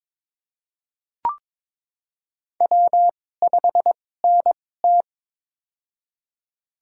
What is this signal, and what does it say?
Morse code, heard as a keyed tone, about 22 wpm, spaced as for 16 wpm, 700 Hz. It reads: W5NT
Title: USA Call Signs - 4 Characters - 2x Character Spacing - 22wpm